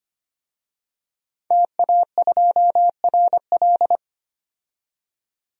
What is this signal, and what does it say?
Morse code, heard as a keyed tone, about 25 wpm, 700 Hz. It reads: TA2RL
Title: DXCC-100 Call Signs - 5 Characters - 25wpm